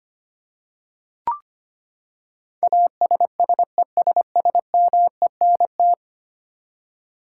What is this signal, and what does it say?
Morse code, heard as a keyed tone, about 25 wpm, 700 Hz. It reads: ASSESSMENT
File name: Ten Letter Words - 25wpm